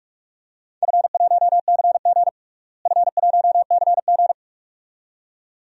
Morse code, heard as a keyed tone, about 46 wpm, 700 Hz. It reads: F1CG F1CG
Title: DXCC-100 Call Signs - 4 Characters - Repeated 2x - 45wpm